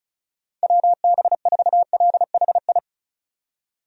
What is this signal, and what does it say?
Morse code, heard as a keyed tone, about 36 wpm, 700 Hz. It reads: WB4LHI
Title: USA Call Signs - 6 Characters - 35wpm